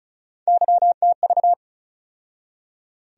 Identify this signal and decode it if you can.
Morse code, heard as a keyed tone, about 36 wpm, 700 Hz. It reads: YTV
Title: USA Call Sign Postfixes - 3 Letters 35wpm